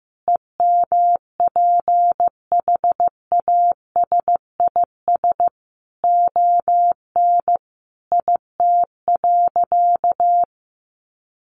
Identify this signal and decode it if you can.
Morse code, heard as a keyed tone, about 15 wpm, 700 Hz. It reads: EMPHASIS ON IT.
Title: Sentences from Top 1400 Words 15wpm